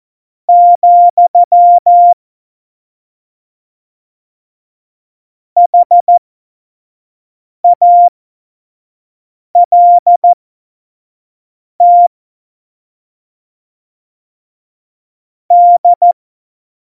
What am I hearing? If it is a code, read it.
Morse code, heard as a keyed tone, about 14 wpm, spaced as for 5 wpm, 700 Hz. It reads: , HALT D
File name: Morsekurs / Witze / Witz 16. Oktober 2018, 25 F70